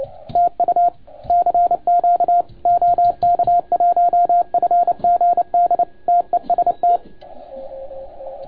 Morse code, heard as a keyed tone, about 29 wpm, 690 Hz, with radio noise. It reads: TU CQ OK1FGD TEST
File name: Archiv / CUC287 / OK1FGD CUC 287